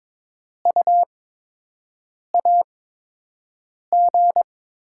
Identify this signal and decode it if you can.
Morse code, heard as a keyed tone, about 22 wpm, spaced as for 6 wpm, 700 Hz. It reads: U AG
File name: Top 20 QSO Elements - 8x Character Spacing - 22wpm